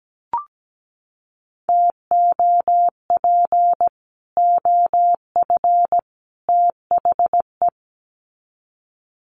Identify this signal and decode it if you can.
Morse code, heard as a keyed tone, about 17 wpm, 700 Hz. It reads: TOP OF THE
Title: Sets of 3 Words 17wpm